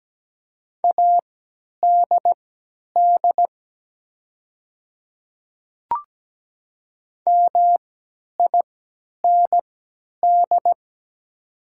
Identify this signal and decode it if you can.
Morse code, heard as a keyed tone, about 17 wpm, spaced as for 10 wpm, 700 Hz. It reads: ADD MIND
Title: 3x Character Spacing - Top 500 Words - 17wpm